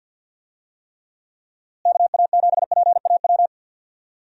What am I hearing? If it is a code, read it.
Morse code, heard as a keyed tone, about 52 wpm, 700 Hz. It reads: KA7PAW
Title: USA Call Signs - 6 Characters - Repeated 3x - 50wpm